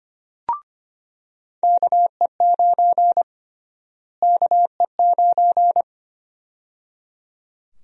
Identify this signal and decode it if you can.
Morse code, heard as a keyed tone, about 25 wpm, 700 Hz. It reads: KE9 KE9
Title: USA Call Sign Prefixes - Repeated 2x - 25wpm